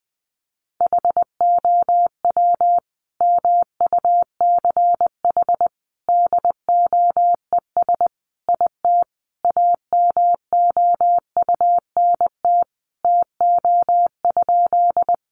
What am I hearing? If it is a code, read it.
Morse code, heard as a keyed tone, about 20 wpm, 700 Hz. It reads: HOW MUCH DOES IT AMOUNT TO?
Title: Sentences from Top 700 Words 20wpm